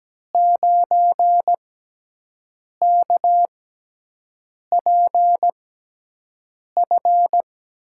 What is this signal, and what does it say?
Morse code, heard as a keyed tone, about 17 wpm, spaced as for 6 wpm, 700 Hz. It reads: 9KPF
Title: USA Call Signs - 6 Characters - 6x Character Spacing - 17wpm